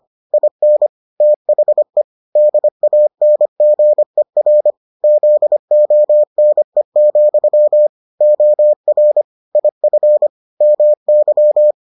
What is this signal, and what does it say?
Morse code, heard as a keyed tone, about 25 wpm, 600 Hz. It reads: IN THE DANGER ZONE, OR IF MY